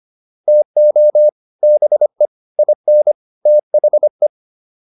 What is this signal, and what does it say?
Morse code, heard as a keyed tone, about 25 wpm, 600 Hz. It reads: TO BE IN THE